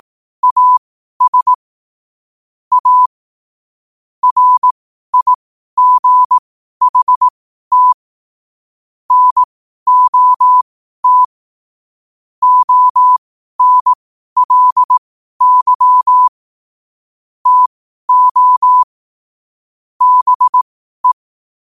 Morse code, heard as a keyed tone, about 18 wpm, spaced as for 12 wpm, 1000 Hz. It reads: AS A RIGHT NOT ONLY TO BE
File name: qotdfiles / 12wpm